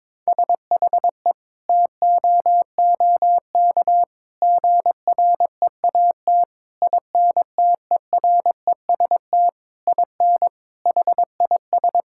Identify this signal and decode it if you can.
Morse code, heard as a keyed tone, about 22 wpm, 700 Hz. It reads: SHE TOOK GREAT INTEREST IN HIS
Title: Sentences from Top 1000 Words - Review Part 6 - 22wpm